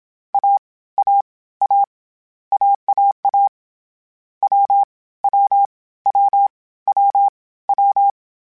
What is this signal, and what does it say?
Morse code, heard as a keyed tone, about 27 wpm, 800 Hz. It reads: A A A AAA W W W W W
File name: Difficult Combos / Difficult Combos-mp3s / AWJ1 @25